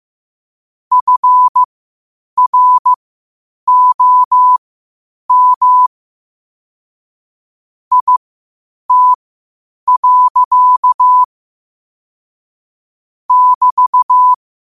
Morse code, heard as a keyed tone, about 15 wpm, spaced as for 8 wpm, 1000 Hz. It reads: FROM IT. <BT>
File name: qotdfiles / 8wpm